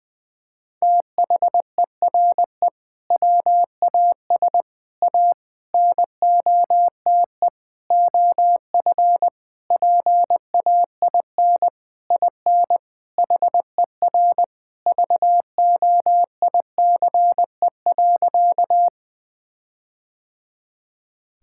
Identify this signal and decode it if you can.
Morse code, heard as a keyed tone, about 20 wpm, 700 Hz. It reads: THERE WAS A NOTE OF PAIN IN HER VOICE.
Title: Sentences from Top 1600 Words 20wpm